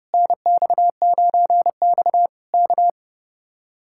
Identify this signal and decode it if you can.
Morse code, heard as a keyed tone, about 28 wpm, 700 Hz. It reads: NX9X K
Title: Speed-Racing Common QSO Phrases - 30wpm